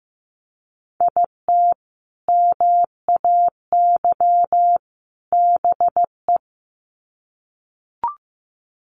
Morse code, heard as a keyed tone, about 15 wpm, 700 Hz. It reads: IT MAY BE
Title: Sets of 3 Words 15wpm